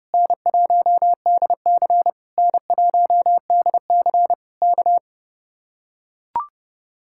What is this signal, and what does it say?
Morse code, heard as a keyed tone, about 31 wpm, 700 Hz. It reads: N1DC N1DC K